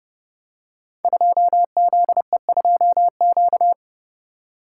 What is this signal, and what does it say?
Morse code, heard as a keyed tone, about 30 wpm, 700 Hz. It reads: 2ZE2Q
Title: Code Groups - 5 Letter:Numbers - 30wpm